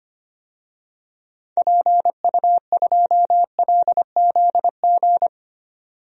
Morse code, heard as a keyed tone, about 25 wpm, 700 Hz. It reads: PU2LZG
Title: DXCC-100 Call Signs - 6 Characters - 25wpm